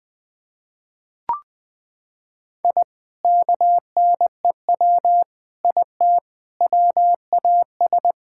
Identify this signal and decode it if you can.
Morse code, heard as a keyed tone, about 20 wpm, 700 Hz. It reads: I KNEW IT WAS
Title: Sentences with 5-Word Phrases - 20wpm